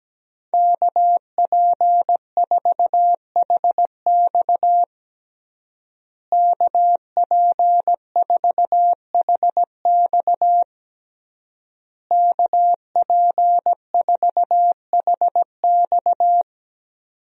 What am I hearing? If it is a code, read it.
Morse code, heard as a keyed tone, about 17 wpm, 700 Hz. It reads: KP4HX KP4HX KP4HX
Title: USA Call Signs - 5 Characters - Repeated 3x - 17wpm